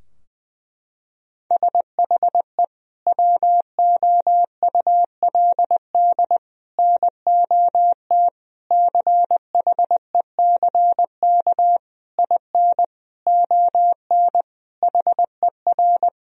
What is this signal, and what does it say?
Morse code, heard as a keyed tone, about 20 wpm, 700 Hz. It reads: SHE WOULD NOT CHECK IN ON HER
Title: Sentences from Top 1000 Words - Review Part 3 - 20wpm